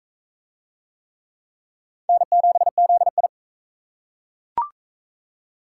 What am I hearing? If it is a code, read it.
Morse code, heard as a keyed tone, about 39 wpm, 700 Hz. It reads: N7ZI